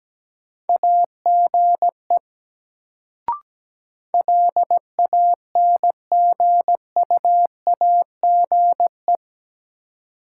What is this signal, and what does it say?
Morse code, heard as a keyed tone, about 17 wpm, 700 Hz. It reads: AGE LANGUAGE
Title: Sentences from Top 500 Words - New Word Drill - 17wpm